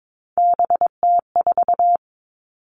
Morse code, heard as a keyed tone, about 22 wpm, 700 Hz. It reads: BT4
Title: Three Challenging Letter-Number Combinations 22wpm